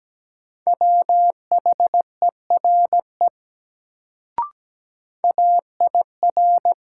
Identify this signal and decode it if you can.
Morse code, heard as a keyed tone, about 17 wpm, 700 Hz. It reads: WHERE AIR
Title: Sets of 2 Words - New Word Drill - 17wpm